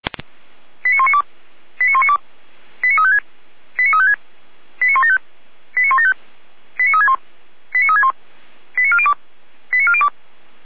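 Five-tone selective call (ZVEI), heard as quick runs of five tones, tones of about 80 ms. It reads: 89192 89192 89356 89356 89175 89175 89361 89361 89402 89402
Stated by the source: ZVEI five-tone selective call 89582 received on an SDR, fire brigade (Feuerwehr) Orlamünde.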